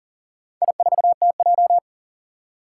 Morse code, heard as a keyed tone, about 41 wpm, 700 Hz. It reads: I4TJ
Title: Four Challenging Letter-Number Combinations 40wpm